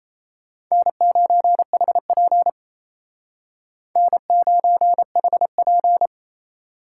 Morse code, heard as a keyed tone, about 33 wpm, 700 Hz. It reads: N9HP N9HP
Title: Speed Racing - USA Call Signs 28wpm